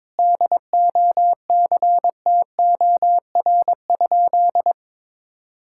Morse code, heard as a keyed tone, about 22 wpm, 700 Hz. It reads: DOCTOR?
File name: practice / 22wpm / Sentences with 4-Word Phrases - 22wpm